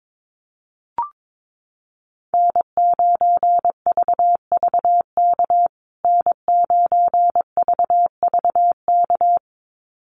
Morse code, heard as a keyed tone, about 22 wpm, 700 Hz. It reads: N9VVK N9VVK
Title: USA Call Signs Repeated 2x 22wpm